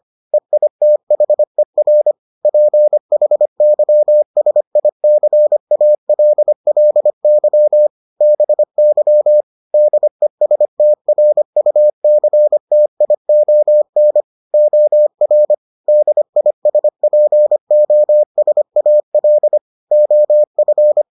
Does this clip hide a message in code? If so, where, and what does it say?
Yes: Morse code from 0.3 to 21.0 s, heard as a keyed tone, about 25 wpm, 600 Hz. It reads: EITHER PHYSICALLY BY DESTRUCTION OR DISPOSAL OF